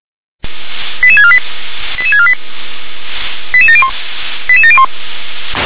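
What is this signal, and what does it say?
Five-tone selective call (ZVEI), heard as quick runs of five tones, tones of about 70 ms. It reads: 88648 88648 88701 88701
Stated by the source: ZVEI five-tone selective call received by SDR.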